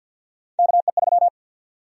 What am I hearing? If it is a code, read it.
Morse code, heard as a keyed tone, about 52 wpm, 700 Hz. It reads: KE3
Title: USA Call Sign Prefixes - 50wpm